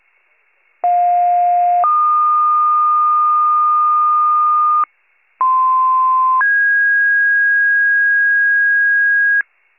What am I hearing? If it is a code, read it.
Two-tone page, heard as a one-second tone, then a three-second tone, about 710 then 1220 Hz, about 1040 then 1650 Hz, with noise. It is heard from 0.8 to 9.4 s.